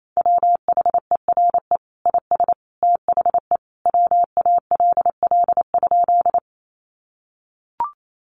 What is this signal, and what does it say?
Morse code, heard as a keyed tone, about 28 wpm, 700 Hz. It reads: WHERE IS THE WALL?
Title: Sentences from Top 1100 Words 28wpm